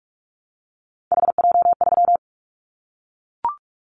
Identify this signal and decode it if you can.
Morse code, heard as a keyed tone, about 44 wpm, 700 Hz. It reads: HJ3
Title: Callsign Prefixes for Top 100 DXCC - 45wpm